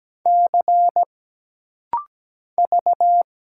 Morse code, heard as a keyed tone, about 17 wpm, 700 Hz. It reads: C V
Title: Single Letter-Number Alt - Rapid-Fire 17wpm